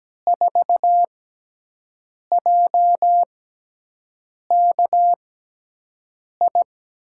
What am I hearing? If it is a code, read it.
Morse code, heard as a keyed tone, about 17 wpm, spaced as for 6 wpm, 700 Hz. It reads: 4JKI
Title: USA Call Signs - 5 Characters - 6x Character Spacing - 17wpm